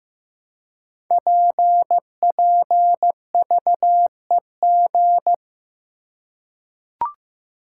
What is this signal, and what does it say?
Morse code, heard as a keyed tone, about 15 wpm, 700 Hz. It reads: PPVEG
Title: Code Groups - 5 Letter:Numbers - 15wpm